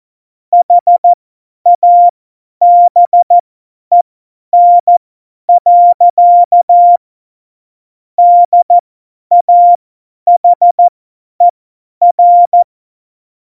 Morse code, heard as a keyed tone, about 14 wpm, spaced as for 10 wpm, 700 Hz. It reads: HABEN. DAHER
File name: Morsekurs / Witze / Witz 06. Oktober 2020, 50 F70